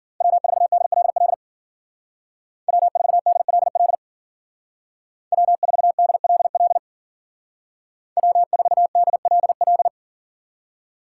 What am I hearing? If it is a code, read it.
Morse code, heard as a keyed tone, about 50 wpm, 700 Hz. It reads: W4DLL W4DLL W4DLL W4DLL
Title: Speed Racing - USA Call Signs 40wpm